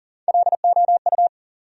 Morse code, heard as a keyed tone, about 39 wpm, 700 Hz. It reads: LOU
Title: USA Common Names - Max 4 Letters - 40wpm